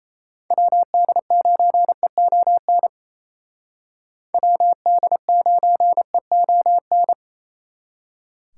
Morse code, heard as a keyed tone, about 34 wpm, 700 Hz. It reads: WD9EON WD9EON
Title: Speed Racing - USA Call Signs 28wpm